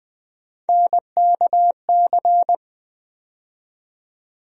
Morse code, heard as a keyed tone, about 21 wpm, 700 Hz. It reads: NKC
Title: USA Call Sign Postfixes - 3 Letters 20wpm